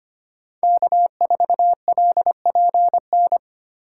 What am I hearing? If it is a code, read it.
Morse code, heard as a keyed tone, about 25 wpm, 700 Hz. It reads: K4LPN
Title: USA Call Signs - 5 Characters - 2x Character Spacing - 25wpm